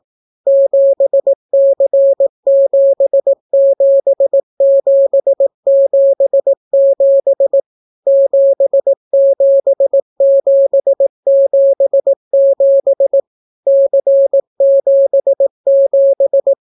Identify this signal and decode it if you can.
Morse code, heard as a keyed tone, about 18 wpm, 550 Hz. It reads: 7C77777 77777 C77